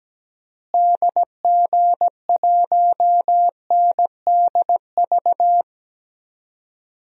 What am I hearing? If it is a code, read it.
Morse code, heard as a keyed tone, about 17 wpm, 700 Hz. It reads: DG1NDV
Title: DXCC-100 Call Signs - 6 Characters - Repeated 3x - 17wpm